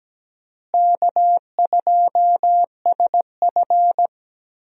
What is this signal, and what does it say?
Morse code, heard as a keyed tone, about 17 wpm, 700 Hz. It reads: K2SF
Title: USA Call Signs - 4 Characters - Repeated 3x - 17wpm